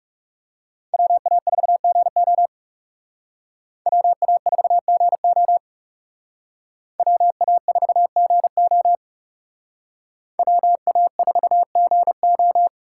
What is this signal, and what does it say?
Morse code, heard as a keyed tone, about 36 wpm, 700 Hz. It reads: WA4GO WA4GO WA4GO WA4GO
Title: Speed Racing - USA 2x2 Call Signs 30wpm